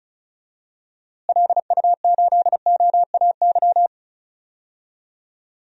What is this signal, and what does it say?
Morse code, heard as a keyed tone, about 36 wpm, 700 Hz. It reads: LU8OAY
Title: DXCC-100 Call Signs - 6 Characters - Repeated 3x - 35wpm